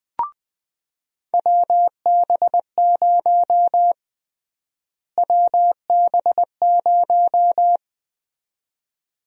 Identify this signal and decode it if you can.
Morse code, heard as a keyed tone, about 20 wpm, 700 Hz. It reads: WB0 WB0
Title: USA Call Sign Prefixes - Repeated 2x - 20wpm